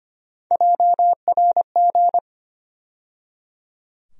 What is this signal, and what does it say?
Morse code, heard as a keyed tone, about 25 wpm, 700 Hz. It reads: JRG
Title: USA Call Sign Postfixes - 3 Letters 25wpm